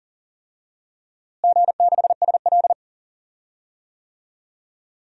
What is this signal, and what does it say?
Morse code, heard as a keyed tone, about 37 wpm, 700 Hz. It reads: G6SL
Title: DXCC-100 Call Signs - 4 Characters - Repeated 2x - 40wpm